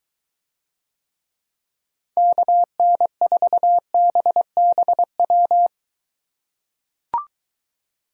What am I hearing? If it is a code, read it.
Morse code, heard as a keyed tone, about 23 wpm, 700 Hz. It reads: KN4BBW